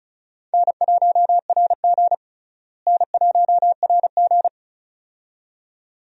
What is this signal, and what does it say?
Morse code, heard as a keyed tone, about 36 wpm, 700 Hz. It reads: N1RG N1RG
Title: USA Call Signs - 4 Characters - Repeated 2x - 35wpm